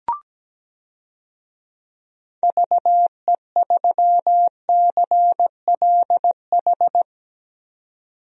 Morse code, heard as a keyed tone, about 17 wpm, 700 Hz. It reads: VE3CLH
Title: DXCC-100 Call Signs - 6 Characters - 17wpm